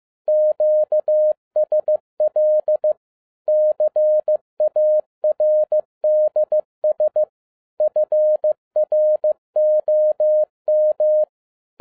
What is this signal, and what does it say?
Morse code, heard as a keyed tone, about 15 wpm, 610 Hz. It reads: QSL CARDS FROM
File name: 2011-10-mp3 / n07-A DOT 15wpm.mp3